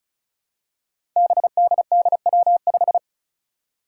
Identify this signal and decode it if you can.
Morse code, heard as a keyed tone, about 34 wpm, 700 Hz. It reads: BDDW5
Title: Five Challenging Letter-Number Combinations 35wpm